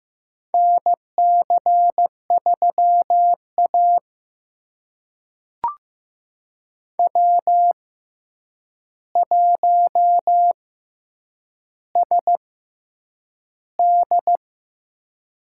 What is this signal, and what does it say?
Morse code, heard as a keyed tone, about 15 wpm, spaced as for 8 wpm, 700 Hz. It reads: NC3A W1SD
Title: USA Call Signs - 4 Characters - 6x Character Spacing - 15wpm